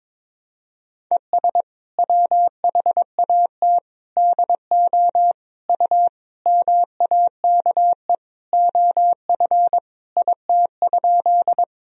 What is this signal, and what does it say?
Morse code, heard as a keyed tone, about 22 wpm, 700 Hz. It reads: ES WHAT DO U MAKE OF IT?
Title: Sentences from Top 100 Words and Abbreviations 22wpm